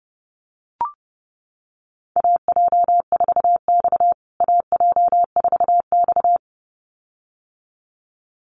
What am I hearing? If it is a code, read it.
Morse code, heard as a keyed tone, about 30 wpm, 700 Hz. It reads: AJ4X AJ4X
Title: USA Call Signs Repeated 2x 30wpm